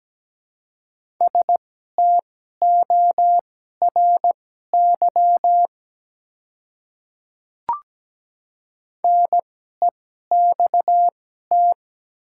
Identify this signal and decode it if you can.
Morse code, heard as a keyed tone, about 17 wpm, spaced as for 12 wpm, 700 Hz. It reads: STORY NEXT
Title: 2x Character Spacing - Top 300 Words - 17wpm